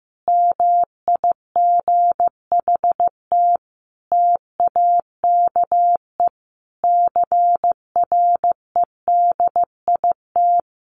Morse code, heard as a keyed tone, about 15 wpm, 700 Hz. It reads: MIGHT TAKE CREDIT
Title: Sentences from Top 1000 Words 15wpm